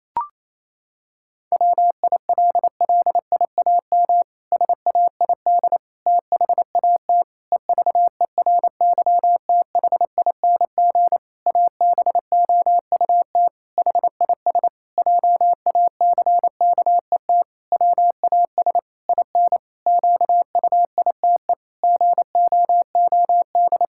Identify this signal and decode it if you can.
Morse code, heard as a keyed tone, about 28 wpm, 700 Hz. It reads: WILLIAM SAID THAT EVERYTHING ABOUT HIS JACKET WAS IN QUITE GOOD
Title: Pangram Challenge v2 - 28wpm